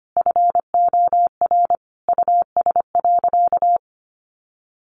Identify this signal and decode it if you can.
Morse code, heard as a keyed tone, about 25 wpm, 700 Hz. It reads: FOR US.
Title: Sentences from Top 900 Words 25wpm